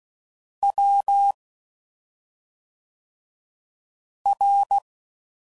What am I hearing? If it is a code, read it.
Morse code, heard as a keyed tone, about 16 wpm, 800 Hz. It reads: W R
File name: Suffixes / Special ING's / SlowToFaster ING's